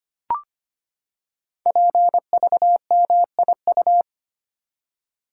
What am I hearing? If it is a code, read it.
Morse code, heard as a keyed tone, about 25 wpm, 700 Hz. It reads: PVMIU
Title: Code Groups - 5 Letter:Numbers - 25wpm